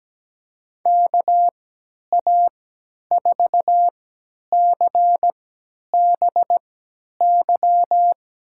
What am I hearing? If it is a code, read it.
Morse code, heard as a keyed tone, about 17 wpm, spaced as for 10 wpm, 700 Hz. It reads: KA4CBY
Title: USA Call Signs - 6 Characters - 3x Character Spacing - 17wpm